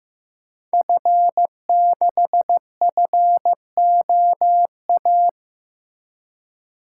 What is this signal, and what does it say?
Morse code, heard as a keyed tone, about 15 wpm, 700 Hz. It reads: F6FOA
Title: Code Groups - 5 Letter:Numbers - 15wpm